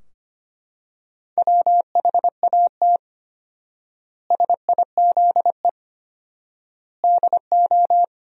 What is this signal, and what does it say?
Morse code, heard as a keyed tone, about 25 wpm, 700 Hz. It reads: WHAT SIZE DO
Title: 4x Word-Spacing - Sentences from Top 400 Words - 25wpm